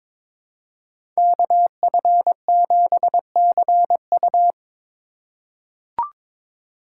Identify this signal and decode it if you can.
Morse code, heard as a keyed tone, about 22 wpm, 700 Hz. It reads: KF7CU
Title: USA Call Signs - 4x Character Spacing - 22wpm